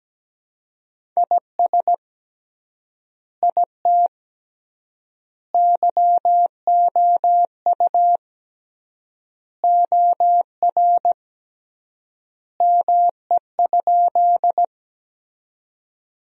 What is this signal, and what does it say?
Morse code, heard as a keyed tone, about 17 wpm, 700 Hz. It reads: IS IT YOU OR ME?
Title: 3x Word-Spacing - Sentences from Top 200 Words - 17wpm